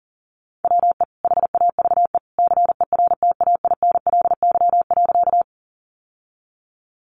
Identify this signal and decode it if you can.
Morse code, heard as a keyed tone, about 40 wpm, 700 Hz. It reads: WE HAVE CERTAINLY.
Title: Sentences from Top 900 Words 40wpm